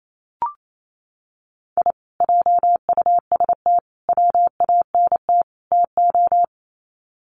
Morse code, heard as a keyed tone, about 28 wpm, 700 Hz. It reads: I JUST WANT TO
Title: Sets of 4 Words 28wpm